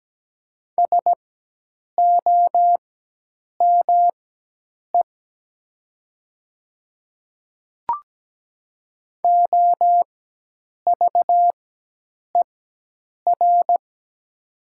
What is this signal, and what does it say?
Morse code, heard as a keyed tone, about 17 wpm, spaced as for 8 wpm, 700 Hz. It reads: SOME OVER
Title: 4x Character Spacing - Top 100 Words - 17wpm